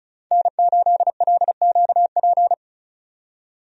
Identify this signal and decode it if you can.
Morse code, heard as a keyed tone, about 36 wpm, 700 Hz. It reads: N8LQP
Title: USA Call Signs - 5 Characters - 35wpm